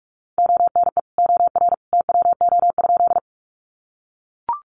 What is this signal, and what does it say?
Morse code, heard as a keyed tone, about 44 wpm, 700 Hz. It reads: ONE OR TWO?
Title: Sentences from Top 100 Words 45wpm